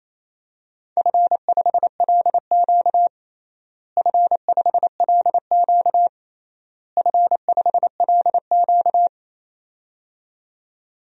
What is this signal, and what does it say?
Morse code, heard as a keyed tone, about 28 wpm, 700 Hz. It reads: F5LQ F5LQ F5LQ
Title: DXCC-100 Call Signs - 4 Characters - Repeated 3x - 28wpm